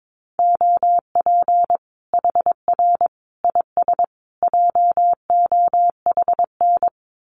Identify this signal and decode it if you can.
Morse code, heard as a keyed tone, about 22 wpm, 700 Hz. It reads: OP HR IS JOHN
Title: Common QSO Phrases 22wpm